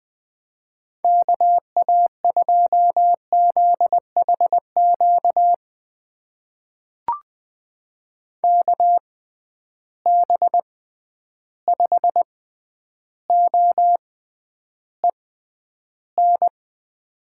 Morse code, heard as a keyed tone, about 20 wpm, spaced as for 10 wpm, 700 Hz. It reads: KA2ZHQ KB5OEN
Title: USA Call Signs - 6 Characters - 6x Character Spacing - 20wpm